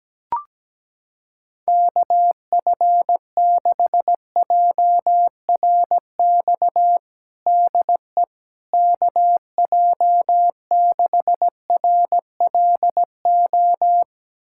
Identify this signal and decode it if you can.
Morse code, heard as a keyed tone, about 17 wpm, 700 Hz. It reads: KF6JRX DE KJ6RLO
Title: USA Callsign DE Callsign 17wpm